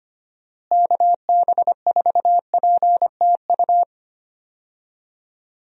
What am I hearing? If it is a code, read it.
Morse code, heard as a keyed tone, about 25 wpm, 700 Hz. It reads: KB4PTU
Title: USA Call Signs - 6 Characters - Repeated 2x - 25wpm